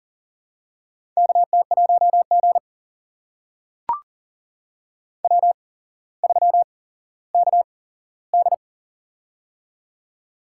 Morse code, heard as a keyed tone, about 40 wpm, spaced as for 17 wpm, 700 Hz. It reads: KT1G W3KD
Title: USA Call Signs - 4 Characters - 8x Character Spacing - 40wpm